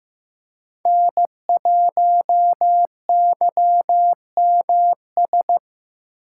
Morse code, heard as a keyed tone, about 15 wpm, 700 Hz. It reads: N1YMS
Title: USA Call Signs - 5 Characters - 2x Character Spacing - 15wpm